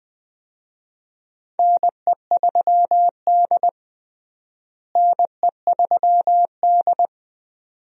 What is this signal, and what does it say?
Morse code, heard as a keyed tone, about 20 wpm, 700 Hz. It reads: NE3D NE3D
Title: USA Call Signs - 4 Characters - Repeated 2x - 20wpm